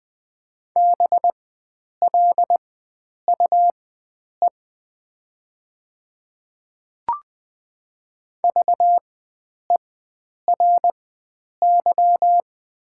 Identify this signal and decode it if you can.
Morse code, heard as a keyed tone, about 20 wpm, spaced as for 9 wpm, 700 Hz. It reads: BLUE VERY
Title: 4x Character Spacing - Top 500 Words - 20wpm